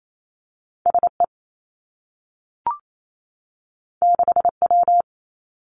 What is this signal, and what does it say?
Morse code, heard as a keyed tone, about 28 wpm, 700 Hz. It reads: S E 6 W
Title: Two Challenging Letter-Number Combinations 28wpm